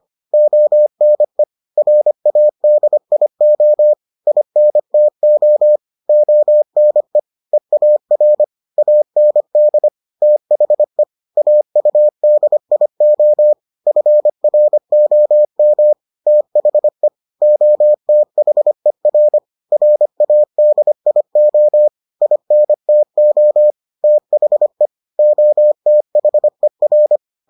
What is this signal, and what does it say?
Morse code, heard as a keyed tone, about 25 wpm, 600 Hz. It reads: ONE RADIO INTO ONE EAR AND THE AUDIO FROM THE OTHER RADIO INTO THE OTHER